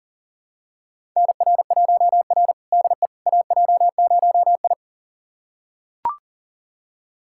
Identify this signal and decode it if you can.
Morse code, heard as a keyed tone, about 37 wpm, 700 Hz. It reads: NR1R DE AJ0I
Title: USA Callsign DE Callsign - 4 Characters - 40wpm